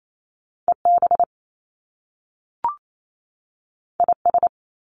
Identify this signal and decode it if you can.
Morse code, heard as a keyed tone, about 28 wpm, 700 Hz. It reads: EB IS
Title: Top Two Letter Combinations in Words 28wpm